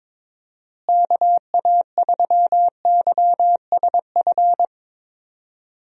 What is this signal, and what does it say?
Morse code, heard as a keyed tone, about 22 wpm, 700 Hz. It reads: KA3YSF
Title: USA Call Signs - 6 Characters - Repeated 3x - 22wpm